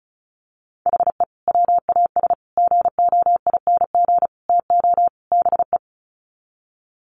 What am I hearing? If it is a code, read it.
Morse code, heard as a keyed tone, about 34 wpm, 700 Hz. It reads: HE WAS GOING TO BE
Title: Sets of 5 Words 35wpm